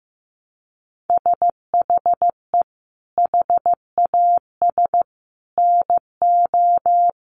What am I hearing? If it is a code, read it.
Morse code, heard as a keyed tone, about 14 wpm, 700 Hz. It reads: SHE HAS NO
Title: Sentences from Top 1300 Words 15wpm